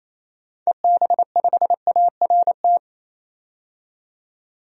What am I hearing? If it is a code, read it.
Morse code, heard as a keyed tone, about 28 wpm, 700 Hz. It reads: EB5ART
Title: DXCC-100 Call Signs - 6 Characters - 28wpm